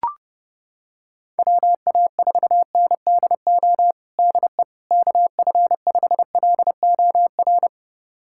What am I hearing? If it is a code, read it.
Morse code, heard as a keyed tone, about 30 wpm, 700 Hz. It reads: WA4NDO DE KF5LOR
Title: USA Callsign DE Callsign 30wpm